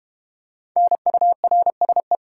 Morse code, heard as a keyed tone, about 32 wpm, 700 Hz. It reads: NURSE